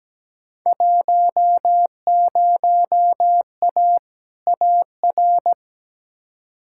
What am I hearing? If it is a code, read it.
Morse code, heard as a keyed tone, about 17 wpm, 700 Hz. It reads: 10A AR
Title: Field Day Exchanges Encore - 17wpm